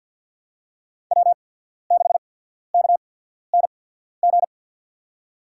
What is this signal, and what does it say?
Morse code, heard as a keyed tone, about 52 wpm, spaced as for 14 wpm, 700 Hz. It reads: W6KNG
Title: USA Call Signs - 5 Characters - 8x Character Spacing - 50wpm